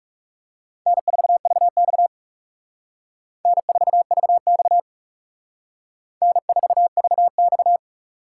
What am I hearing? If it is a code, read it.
Morse code, heard as a keyed tone, about 37 wpm, 700 Hz. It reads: N4VX N4VX N4VX
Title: Speed Racing - USA Call Signs 30wpm